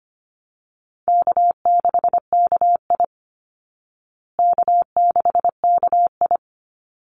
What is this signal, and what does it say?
Morse code, heard as a keyed tone, about 25 wpm, 700 Hz. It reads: K6KI K6KI
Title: Easy Call Signs Repeated 2x - 25wpm